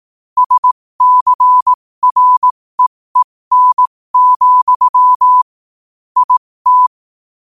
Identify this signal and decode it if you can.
Morse code, heard as a keyed tone, about 18 wpm, spaced as for 15 wpm, 1000 Hz. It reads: SCREEN, IT
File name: qotdfiles / 15wpm